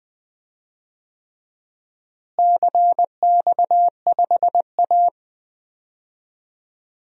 Morse code, heard as a keyed tone, about 21 wpm, 700 Hz. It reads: CX5A
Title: DXCC-100 Call Signs - 4 Characters - Repeated 3x - 20wpm